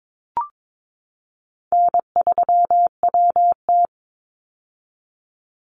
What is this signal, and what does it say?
Morse code, heard as a keyed tone, about 22 wpm, 700 Hz. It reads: N3WT
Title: Easy Call Signs - 22wpm